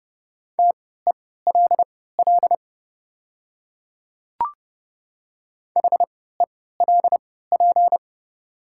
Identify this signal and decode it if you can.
Morse code, heard as a keyed tone, about 30 wpm, spaced as for 17 wpm, 700 Hz. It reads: TELL HELP
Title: 3x Character Spacing - Top 200 Words - 30wpm